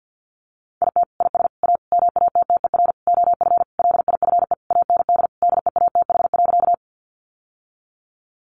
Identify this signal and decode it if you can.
Morse code, heard as a keyed tone, about 48 wpm, 700 Hz. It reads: IT IS A MATTER OF LIFE AND DEATH.
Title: Sentences from Top 1100 Words 50wpm